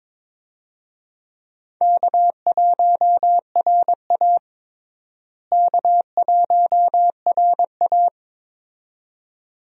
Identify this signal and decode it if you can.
Morse code, heard as a keyed tone, about 22 wpm, 700 Hz. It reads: K1RA K1RA
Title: USA Call Signs - 4 Characters - Repeated 2x - 22wpm